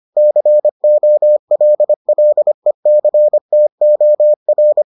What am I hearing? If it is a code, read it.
Morse code, heard as a keyed tone, about 25 wpm, 600 Hz. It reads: COLLECTOR